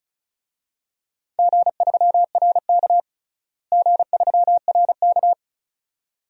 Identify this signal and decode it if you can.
Morse code, heard as a keyed tone, about 34 wpm, 700 Hz. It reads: G3RK G3RK
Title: DXCC-100 Call Signs - 4 Characters - Repeated 2x - 35wpm